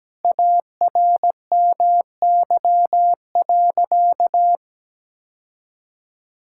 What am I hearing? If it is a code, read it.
Morse code, heard as a keyed tone, about 17 wpm, 700 Hz. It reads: ARMY.
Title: Sentences from Top 1000 Words - Review Part 5 - 17wpm